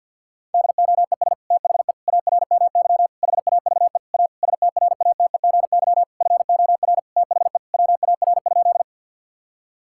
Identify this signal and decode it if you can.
Morse code, heard as a keyed tone, about 52 wpm, 690 Hz. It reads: DOES THE ARMY HAVE A STRATEGY FOR THE WAR?
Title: Sentences from Top 1000 Words - Review Part 5 - 50wpm